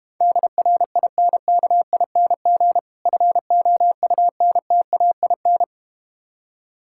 Morse code, heard as a keyed tone, about 32 wpm, 700 Hz. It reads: DRINKING FOUNTAIN